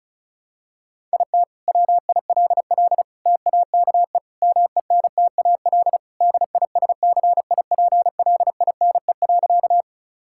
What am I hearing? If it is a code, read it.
Morse code, heard as a keyed tone, about 34 wpm, 690 Hz. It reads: IT WILL TAKE MENTAL DISCIPLINE.
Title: Sentences from Top 1000 Words - Review Part 6 - 35wpm